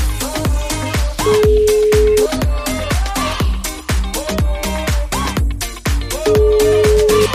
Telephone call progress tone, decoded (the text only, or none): ringback tone